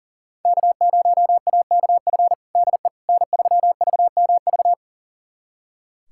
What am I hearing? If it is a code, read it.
Morse code, heard as a keyed tone, about 40 wpm, 700 Hz. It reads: K0AKF DE N3VMV
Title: USA Callsign DE Callsign - 5 Characters - 40wpm